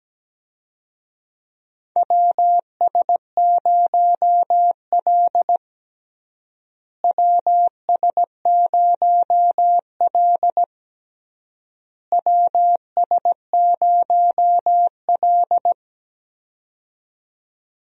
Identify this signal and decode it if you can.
Morse code, heard as a keyed tone, about 17 wpm, 700 Hz. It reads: WS0L WS0L WS0L
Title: USA Call Signs - 4 Characters - Repeated 3x - 17wpm